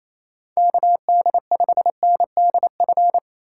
Morse code, heard as a keyed tone, about 28 wpm, 700 Hz. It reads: KD5NDF
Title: USA Call Signs - 2x Character Spacing - 28wpm